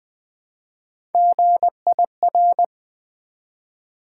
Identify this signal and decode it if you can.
Morse code, heard as a keyed tone, about 20 wpm, 700 Hz. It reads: GIR
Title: Build A Word - 1 Letter At A Time - 3 to 5-Letter Words - 20wpm